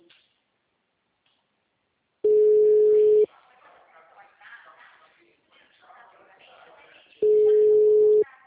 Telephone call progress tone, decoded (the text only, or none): ringback tone